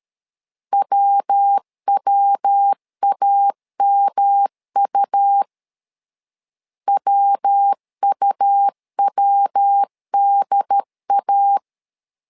Morse code, heard as a keyed tone, about 13 wpm, 800 Hz. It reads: WWAMU WUWDA
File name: LEZIONI / 3 MWRO / 3 MW 12wpm 1633